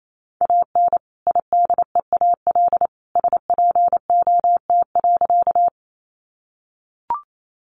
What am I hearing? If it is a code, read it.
Morse code, heard as a keyed tone, about 28 wpm, 700 Hz. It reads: AN IDEAL SPOT.
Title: Sentences from Top 1400 Words 28wpm